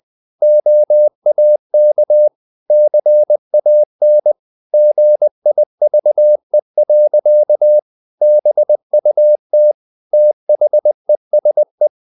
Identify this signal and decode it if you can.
Morse code, heard as a keyed tone, about 20 wpm, 600 Hz. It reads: OAK CAN GIVE. BUT THESE
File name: the old man and the sea / the old man and the sea part 1 20wpm